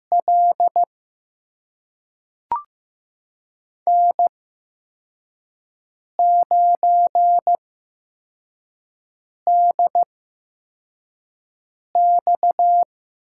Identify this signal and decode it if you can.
Morse code, heard as a keyed tone, about 15 wpm, spaced as for 4 wpm, 700 Hz. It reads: L N9DX
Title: USA Call Signs - 5 Characters - 8x Character Spacing - 15wpm